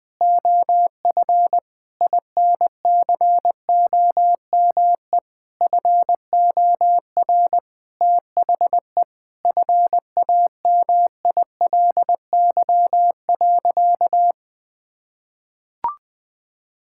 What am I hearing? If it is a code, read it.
Morse code, heard as a keyed tone, about 21 wpm, 700 Hz. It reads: OF INCOME FOR THE FAMILY.
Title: Sentences from Top 1000 Words - Review Part 5 - 20wpm